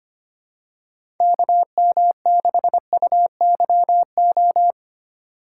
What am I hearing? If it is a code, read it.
Morse code, heard as a keyed tone, about 25 wpm, 700 Hz. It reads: KM6UYO
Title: USA Call Signs - 6 Characters - 4x Character Spacing - 25wpm